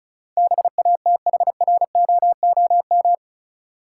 Morse code, heard as a keyed tone, about 34 wpm, 700 Hz. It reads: BATHROOM
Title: Sets of 5 Words Encore - New Word Drill - 35wpm